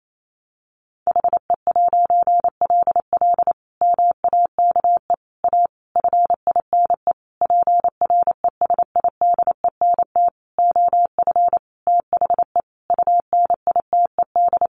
Morse code, heard as a keyed tone, about 28 wpm, 700 Hz. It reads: HE'LL MAKE A FINE PRESIDENT OF THE UNITED
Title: A Sound Of Thunder 28wpm